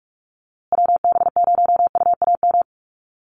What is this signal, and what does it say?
Morse code, heard as a keyed tone, about 44 wpm, 700 Hz. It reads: WB0UAM
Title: USA Call Signs Repeated 2x 45wpm